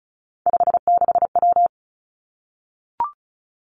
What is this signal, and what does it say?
Morse code, heard as a keyed tone, about 36 wpm, 700 Hz. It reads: 56W
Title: Three Challenging Letter-Number Combinations 35wpm